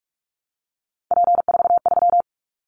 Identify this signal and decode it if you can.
Morse code, heard as a keyed tone, about 44 wpm, 700 Hz. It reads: P43
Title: Callsign Prefixes for Top 100 DXCC - 45wpm